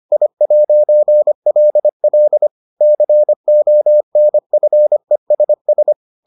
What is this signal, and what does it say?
Morse code, heard as a keyed tone, about 25 wpm, 600 Hz. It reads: I'LL CONFESS